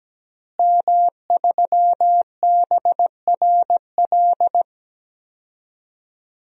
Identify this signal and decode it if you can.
Morse code, heard as a keyed tone, about 17 wpm, 700 Hz. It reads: M3BRL
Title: DXCC-100 Call Signs - 5 Characters - Repeated 3x - 17wpm